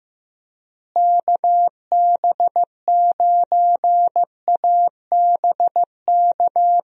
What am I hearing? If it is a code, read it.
Morse code, heard as a keyed tone, about 15 wpm, 700 Hz. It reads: KB9ABK
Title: USA Call Signs - 6 Characters - Repeated 3x - 15wpm